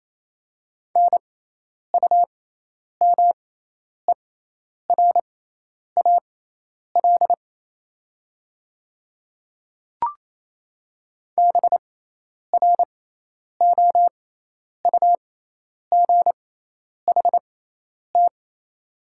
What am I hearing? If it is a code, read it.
Morse code, heard as a keyed tone, about 28 wpm, spaced as for 10 wpm, 690 Hz. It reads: NUMERAL BROUGHT
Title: 6x Character Spacing - Top 500 Words - 28wpm